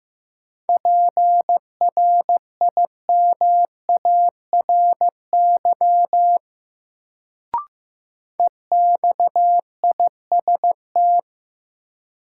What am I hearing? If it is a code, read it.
Morse code, heard as a keyed tone, about 15 wpm, 700 Hz. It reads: PRIMARY EXIST
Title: Sentences from Top 1000 Words - Review Part 5 - New Word Drill - 15wpm